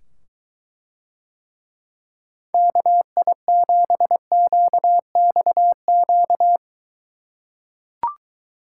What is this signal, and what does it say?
Morse code, heard as a keyed tone, about 23 wpm, 700 Hz. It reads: KI7QXQ